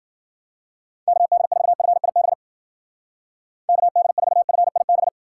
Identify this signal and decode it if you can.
Morse code, heard as a keyed tone, about 51 wpm, 700 Hz. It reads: KD4FIB KD4FIB